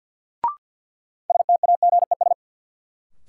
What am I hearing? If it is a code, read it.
Morse code, heard as a keyed tone, about 53 wpm, 700 Hz. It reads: STAGES
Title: Sets of 5 Words Encore - New Word Drill - 50wpm